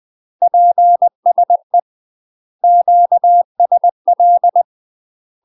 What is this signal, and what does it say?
Morse code, heard as a keyed tone, about 20 wpm, 700 Hz. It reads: PSE QSL
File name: UK MASTER QSOs TRAINING / 20WPM / UK MASTER QSOs 0025